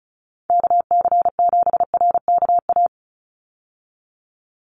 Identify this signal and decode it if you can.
Morse code, heard as a keyed tone, about 34 wpm, 700 Hz. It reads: KC7RKA
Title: USA Call Signs 35wpm